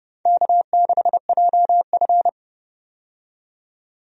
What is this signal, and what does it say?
Morse code, heard as a keyed tone, about 29 wpm, 700 Hz. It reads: K6JF
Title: USA Call Signs - 4 Characters - 30wpm